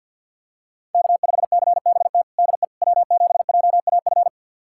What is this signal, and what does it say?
Morse code, heard as a keyed tone, about 48 wpm, 700 Hz. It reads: K5CBT DE W7JAF
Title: USA Callsign DE Callsign - 5 Characters - 50wpm